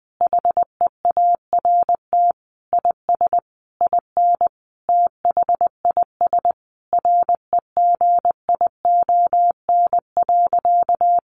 Morse code, heard as a keyed tone, about 20 wpm, 700 Hz. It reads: HEART IS IN THIS REGION.
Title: Sentences from Top 1500 Words 20wpm